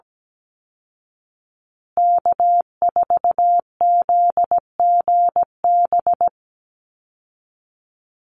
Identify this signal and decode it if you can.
Morse code, heard as a keyed tone, about 17 wpm, 700 Hz. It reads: K4ZGB
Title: Call Signs Repeated 2x - 17wpm